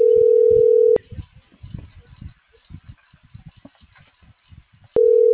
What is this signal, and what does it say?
Ringback tone (North America), 440 + 480 Hz.